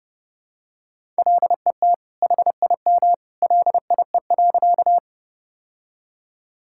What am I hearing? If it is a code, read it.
Morse code, heard as a keyed tone, about 30 wpm, 700 Hz. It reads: LET HIM LIE.
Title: Sentences from Top 1600 Words 30wpm